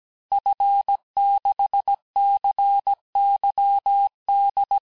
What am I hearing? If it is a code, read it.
Morse code, heard as a keyed tone, about 17 wpm, 800 Hz. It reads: F6CYD
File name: corrige-cours-1581-fin-du-QSO-vitesse-16-mpm